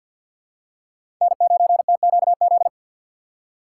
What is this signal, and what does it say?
Morse code, heard as a keyed tone, about 53 wpm, 700 Hz. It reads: N9TQZ
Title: USA Call Signs - 5 Characters - 3x Character Spacing - 50wpm